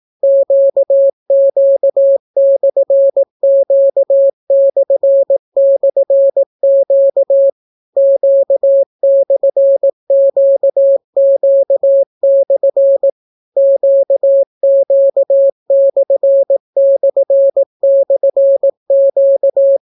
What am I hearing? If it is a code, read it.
Morse code, heard as a keyed tone, about 18 wpm, 550 Hz. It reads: QQ/Q//Q Q/QQ/ QQ///Q